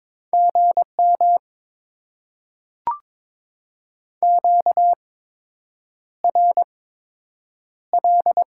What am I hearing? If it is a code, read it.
Morse code, heard as a keyed tone, about 22 wpm, spaced as for 8 wpm, 700 Hz. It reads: GM QRL
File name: Top 20 QSO Elements - 8x Character Spacing - 22wpm